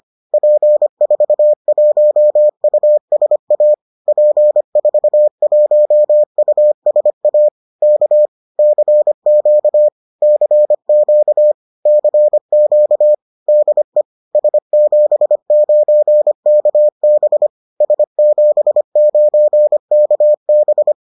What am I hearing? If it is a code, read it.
Morse code, heard as a keyed tone, about 25 wpm, 600 Hz. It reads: P41USA P41USA K CQ CQ CQ DE S79KB S79KB